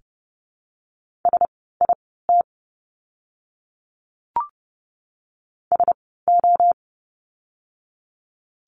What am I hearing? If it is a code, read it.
Morse code, heard as a keyed tone, about 30 wpm, spaced as for 17 wpm, 700 Hz. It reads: SIT SO
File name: course / 30:10wpm / 115 Lesson 19f - Letter w - Words 30wpm